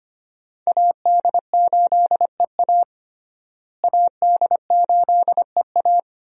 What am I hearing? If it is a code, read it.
Morse code, heard as a keyed tone, about 25 wpm, 700 Hz. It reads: AD8EA AD8EA
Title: USA Call Signs - 5 Characters - Repeated 3x - 25wpm